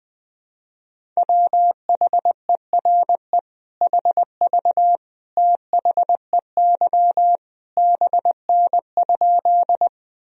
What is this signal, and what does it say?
Morse code, heard as a keyed tone, about 20 wpm, 700 Hz. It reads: WHERE HV THEY BN?
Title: Sentences from Top 200 Words and Abbreviations 20wpm